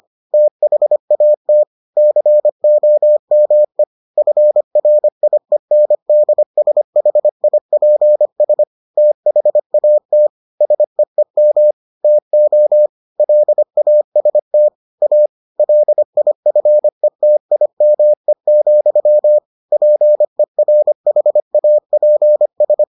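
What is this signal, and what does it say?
Morse code, heard as a keyed tone, about 25 wpm, 600 Hz. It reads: THAT COME FRIENDSHIPS THAT SEEM TO LAST A LIFETIME, PERHAPS
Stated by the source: Morse edition of the Foundations of Amateur Radio episodes.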